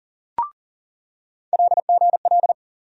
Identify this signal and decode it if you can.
Morse code, heard as a keyed tone, about 40 wpm, 700 Hz. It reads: LGL
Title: USA Call Sign Postfixes - 3 Letters 40wpm